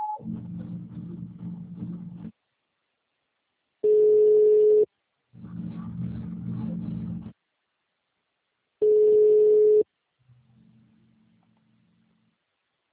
Ringback tone (Europe, 425 Hz).
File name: out- -2021- - - .12